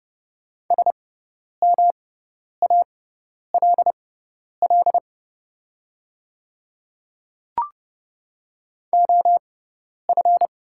Morse code, heard as a keyed tone, about 30 wpm, spaced as for 10 wpm, 700 Hz. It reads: SMALL OF
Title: 6x Character Spacing - Top 200 Words - 30wpm